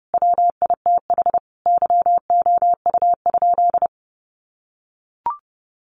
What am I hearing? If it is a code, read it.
Morse code, heard as a keyed tone, about 30 wpm, 700 Hz. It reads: WITH YOU?
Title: Sentences from Top 700 Words 30wpm